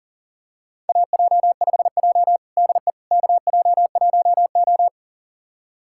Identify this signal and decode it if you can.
Morse code, heard as a keyed tone, about 40 wpm, 700 Hz. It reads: AJ5J DE KJ1O
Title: USA Callsign DE Callsign - 4 Characters - 40wpm